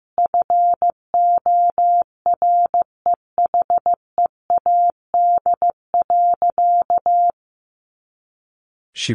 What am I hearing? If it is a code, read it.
Morse code, heard as a keyed tone, about 14 wpm, 700 Hz. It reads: FOREHEAD.
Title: Easy Phrases 15wpm